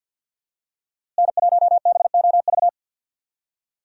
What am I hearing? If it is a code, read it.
Morse code, heard as a keyed tone, about 52 wpm, 700 Hz. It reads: N1BOV
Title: USA Call Signs - 5 Characters - Repeated 2x - 50wpm